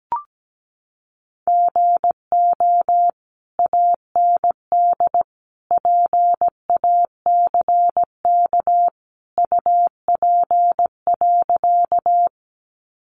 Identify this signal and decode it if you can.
Morse code, heard as a keyed tone, about 17 wpm, 700 Hz. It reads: GO AND PACK UP.
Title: Sentences from Top 1300 Words 17wpm